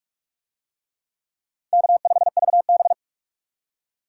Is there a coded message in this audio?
Yes: Morse code from 1.7 to 2.9 s, heard as a keyed tone, about 38 wpm, 700 Hz. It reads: K5VB